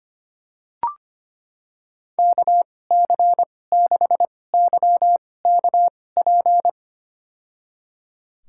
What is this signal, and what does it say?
Morse code, heard as a keyed tone, about 25 wpm, spaced as for 18 wpm, 700 Hz. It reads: KC6YKP
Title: USA Call Signs - 6 Characters - 2x Character Spacing - 25wpm